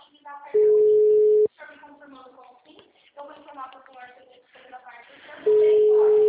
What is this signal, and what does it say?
Ringback tone (Europe, 425 Hz), with speech.